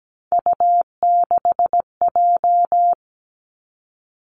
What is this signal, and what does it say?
Morse code, heard as a keyed tone, about 17 wpm, 700 Hz. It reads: U6J
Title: Three Challenging Letter-Number Combinations 17wpm